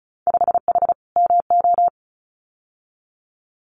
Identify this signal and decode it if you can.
Morse code, heard as a keyed tone, about 36 wpm, 700 Hz. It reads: 5H MO
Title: Winter Field Day Exchanges Encore 35wpm